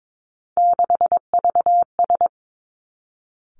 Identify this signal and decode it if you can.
Morse code, heard as a keyed tone, about 22 wpm, 700 Hz. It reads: 6VS
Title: Three Challenging Letter-Number Combinations 22wpm